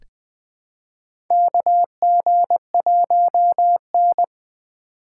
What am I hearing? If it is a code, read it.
Morse code, heard as a keyed tone, about 20 wpm, 700 Hz. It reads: KG1N
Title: Four Letter-Number Combinations - 20wpm